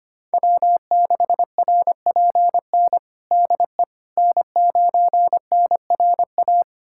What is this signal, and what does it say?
Morse code, heard as a keyed tone, about 25 wpm, 700 Hz. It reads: W6RPN DE N9NRA
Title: USA Callsign DE Callsign - 5 Characters - 25wpm